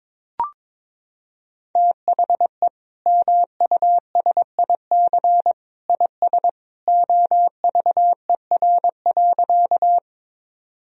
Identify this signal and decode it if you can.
Morse code, heard as a keyed tone, about 22 wpm, 700 Hz. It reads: THE MUSIC IS OVER.
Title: Sentences from Top 1000 Words - Review Part 1 - 22wpm